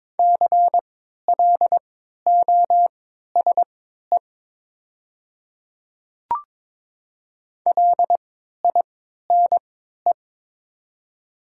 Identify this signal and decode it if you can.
Morse code, heard as a keyed tone, about 23 wpm, spaced as for 13 wpm, 700 Hz. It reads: CLOSE LINE
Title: 3x Character Spacing - Top 300 Words - 22wpm